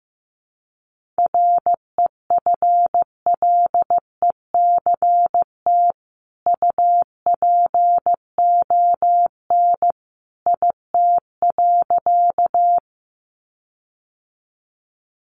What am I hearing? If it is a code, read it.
Morse code, heard as a keyed tone, about 15 wpm, 700 Hz. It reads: REFLECT UPON IT.
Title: Sentences from Top 1500 Words 15wpm